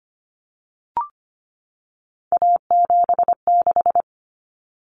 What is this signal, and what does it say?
Morse code, heard as a keyed tone, about 25 wpm, 700 Hz. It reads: A76
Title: Common Cut Numbers - 3-Digit 25wpm